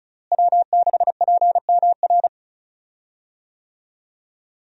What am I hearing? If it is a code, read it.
Morse code, heard as a keyed tone, about 36 wpm, 700 Hz. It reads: W6PMR
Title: USA Call Signs - 5 Characters - Repeated 3x - 35wpm